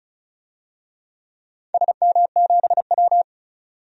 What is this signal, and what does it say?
Morse code, heard as a keyed tone, about 34 wpm, 700 Hz. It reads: SM7W
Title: DXCC-100 Call Signs - 4 Characters - 35wpm